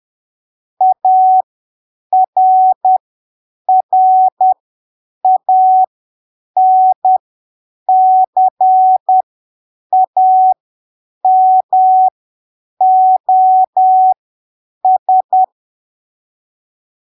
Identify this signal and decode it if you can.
Morse code, heard as a keyed tone, about 10 wpm, spaced as for 7 wpm, 750 Hz. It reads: ARRANCAMOS
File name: arquivos / texto / texto 20 10ppm